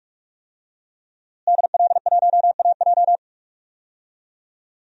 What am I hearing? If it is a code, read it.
Morse code, heard as a keyed tone, about 46 wpm, 700 Hz. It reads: DL1AJ
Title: DXCC-100 Call Signs - 5 Characters - Repeated 3x - 45wpm